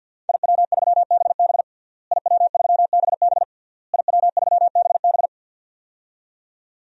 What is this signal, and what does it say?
Morse code, heard as a keyed tone, about 51 wpm, 700 Hz. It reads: IW3BB IW3BB IW3BB
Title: DXCC-100 Call Signs - 5 Characters - Repeated 3x - 50wpm